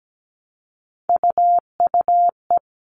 Morse code, heard as a keyed tone, about 17 wpm, 700 Hz. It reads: UUE
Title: All Cut Numbers - 3-Digit 17wpm